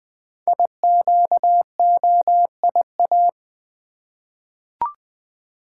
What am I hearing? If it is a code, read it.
Morse code, heard as a keyed tone, about 20 wpm, 700 Hz. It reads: IQOIA
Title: Code Groups - 5 Letters - 20wpm